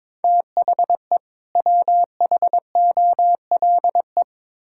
Morse code, heard as a keyed tone, about 22 wpm, 700 Hz. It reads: THE WHOLE
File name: Build A Phrase - 2-Word Phrases - 22wpm